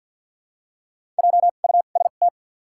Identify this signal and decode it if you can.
Morse code, heard as a keyed tone, about 49 wpm, 700 Hz. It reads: J U S T
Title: 2x Character Spacing - Top 200 Words - 50wpm